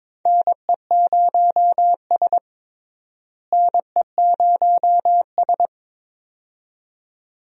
Morse code, heard as a keyed tone, about 22 wpm, 700 Hz. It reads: NE0S NE0S
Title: USA Call Signs - 4 Characters - Repeated 2x - 22wpm